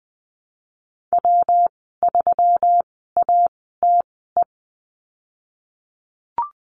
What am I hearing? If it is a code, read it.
Morse code, heard as a keyed tone, about 20 wpm, spaced as for 15 wpm, 700 Hz. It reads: W3ATE
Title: 147 Lesson 24g - Letter y - Call Signs 20wpm